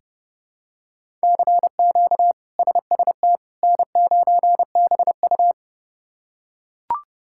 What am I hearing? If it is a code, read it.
Morse code, heard as a keyed tone, about 30 wpm, 700 Hz. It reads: CQ SST N9BU
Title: K1USN SST QSO Practice - 30wpm